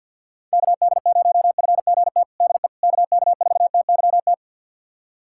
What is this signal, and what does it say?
Morse code, heard as a keyed tone, about 52 wpm, 700 Hz. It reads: KD0FGT DE KK4TYT
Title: USA Callsign DE Callsign - 6 Characters - 50wpm